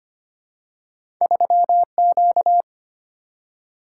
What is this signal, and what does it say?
Morse code, heard as a keyed tone, about 25 wpm, 700 Hz. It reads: 3Q
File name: Two Letter-Number Combinations - 25wpm